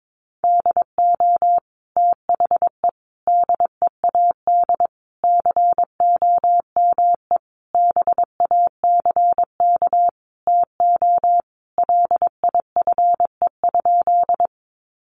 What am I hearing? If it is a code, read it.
Morse code, heard as a keyed tone, about 22 wpm, 700 Hz. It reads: DO THE DEAD COME BACK TO LIFE?
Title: Sentences from Top 1500 Words 22wpm